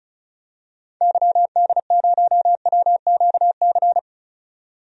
Morse code, heard as a keyed tone, about 34 wpm, 700 Hz. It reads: YD0WQC
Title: DXCC-100 Call Signs - 6 Characters - 35wpm